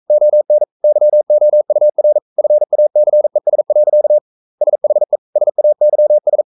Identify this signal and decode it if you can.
Morse code, heard as a keyed tone, about 42 wpm, 600 Hz. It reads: ON YOUR FACES. SHE SAYS
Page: Romanen Never af Ken Follet på engelsk.